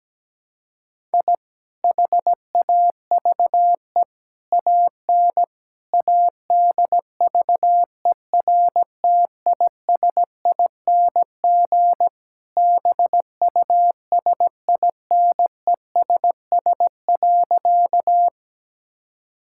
Morse code, heard as a keyed tone, about 17 wpm, 700 Hz. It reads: I HAVE AN ADVERTISING BUSINESS.
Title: Sentences from Top 1000 Words - Review Part 6 - 17wpm